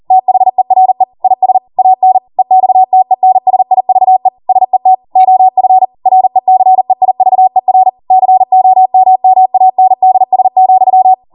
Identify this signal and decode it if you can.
Morse code, heard as a keyed tone, about 40 wpm, 750 Hz. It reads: THERE IS AN EXTENSIVE SET OF RECEIVER COMMANDS,